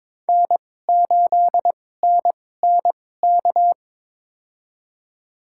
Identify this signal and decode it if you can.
Morse code, heard as a keyed tone, about 21 wpm, spaced as for 16 wpm, 700 Hz. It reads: N8NNK
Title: USA Call Signs - 5 Characters - 2x Character Spacing - 22wpm